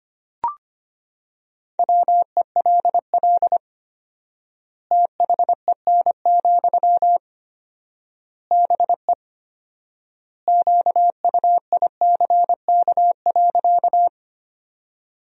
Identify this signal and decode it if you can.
Morse code, heard as a keyed tone, about 25 wpm, 700 Hz. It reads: WELL THEN, BE QUICK.
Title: 4x Word-Spacing - Sentences from Top 500 Words - 25wpm